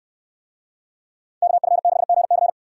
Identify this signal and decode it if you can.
Morse code, heard as a keyed tone, about 69 wpm, 700 Hz. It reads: KF6GC